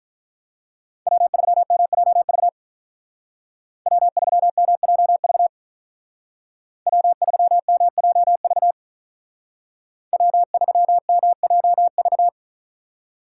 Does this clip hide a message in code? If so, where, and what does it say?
Yes: Morse code from 1.1 to 12.3 s, heard as a keyed tone, about 46 wpm, 700 Hz. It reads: W3MJV W3MJV W3MJV W3MJV